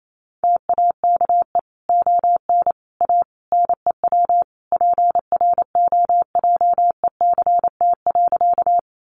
Morse code, heard as a keyed tone, about 28 wpm, 700 Hz. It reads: TAKE ON A NEW PROJECT.
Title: Sentences from Top 1400 Words 28wpm